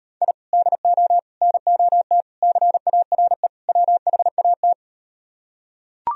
Morse code, heard as a keyed tone, about 38 wpm, 700 Hz. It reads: I DO NOT CARE WHAT